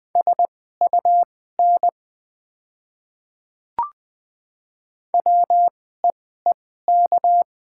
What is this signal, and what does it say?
Morse code, heard as a keyed tone, about 20 wpm, spaced as for 14 wpm, 700 Hz. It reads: SUN WEEK
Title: 2x Character Spacing - Top 500 Words - 20wpm